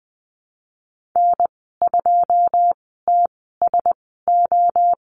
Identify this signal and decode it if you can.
Morse code, heard as a keyed tone, about 20 wpm, spaced as for 14 wpm, 700 Hz. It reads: N2TSO
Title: 082 Lesson 14g - Number 2 - Call Signs 20wpm